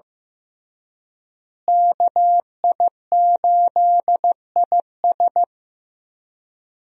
Morse code, heard as a keyed tone, about 15 wpm, 700 Hz. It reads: KI8IS
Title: USA Call Signs - 5 Characters - Repeated 3x - 15wpm